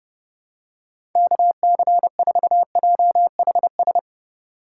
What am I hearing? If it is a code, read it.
Morse code, heard as a keyed tone, about 30 wpm, 700 Hz. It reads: KC4JHS
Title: USA Call Signs - 6 Characters - 2x Character Spacing - 30wpm